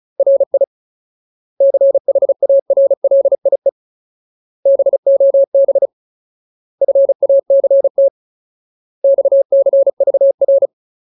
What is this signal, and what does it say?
Morse code, heard as a keyed tone, about 34 wpm, 550 Hz. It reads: RI CHARLIE BOB FACT XCVR
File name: Ultimate Word List / 35WPM Ultimate